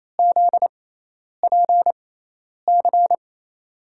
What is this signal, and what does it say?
Morse code, heard as a keyed tone, about 28 wpm, spaced as for 10 wpm, 700 Hz. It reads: ZPC
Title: USA Call Signs - 6 Characters - 6x Character Spacing - 28wpm